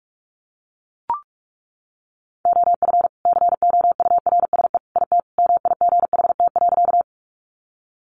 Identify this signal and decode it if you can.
Morse code, heard as a keyed tone, about 44 wpm, 700 Hz. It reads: OF COURSE IT MIGHT.
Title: Sentences from Top 500 Words 45wpm